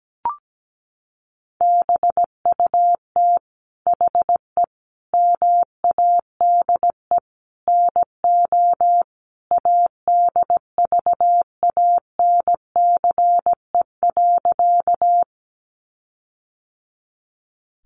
Morse code, heard as a keyed tone, about 17 wpm, 700 Hz. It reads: BUT HE MADE NO ADVANCE.
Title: Sentences from Top 1400 Words 17wpm